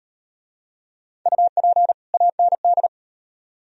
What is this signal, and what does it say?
Morse code, heard as a keyed tone, about 37 wpm, 700 Hz. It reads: UP AND